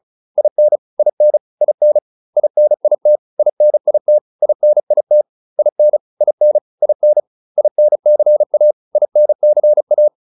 Morse code, heard as a keyed tone, about 34 wpm, 600 Hz. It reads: IN IN IN INIT INIT INIT IN IN IN INCA INCA